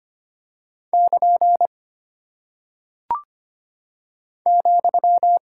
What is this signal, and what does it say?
Morse code, heard as a keyed tone, about 25 wpm, 700 Hz. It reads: <KN> ,
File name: practice / 25wpm / Single Punctuation-Prosign - 25wpm